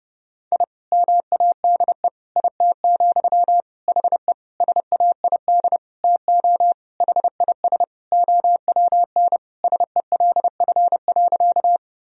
Morse code, heard as a keyed tone, about 32 wpm, 700 Hz. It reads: I MADE IT, HE SAID TO HIS OWN SELF.
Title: Sentences from Top 1000 Words - Review Part 1 - 30wpm